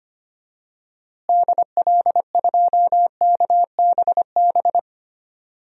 Morse code, heard as a keyed tone, about 25 wpm, 700 Hz. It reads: DL2KBB
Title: DXCC-100 Call Signs - 6 Characters - Repeated 3x - 25wpm